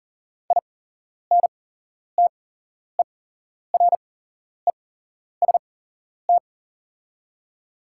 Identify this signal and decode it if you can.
Morse code, heard as a keyed tone, about 37 wpm, spaced as for 11 wpm, 700 Hz. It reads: INTEREST
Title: 8x Character Spacing - Top 500 Words - 40wpm